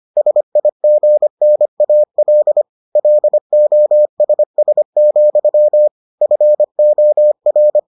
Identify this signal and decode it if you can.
Morse code, heard as a keyed tone, about 25 wpm, 600 Hz. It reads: SIGNAL LOSS, FOR